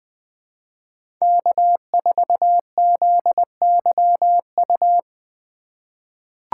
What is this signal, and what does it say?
Morse code, heard as a keyed tone, about 20 wpm, 700 Hz. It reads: K4ZYU
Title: USA Call Signs - 5 Characters - 3x Character Spacing - 20wpm